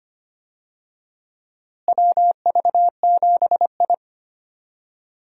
Morse code, heard as a keyed tone, about 25 wpm, 700 Hz. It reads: WV7I
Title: USA Call Signs - 4 Characters - Repeated 3x - 25wpm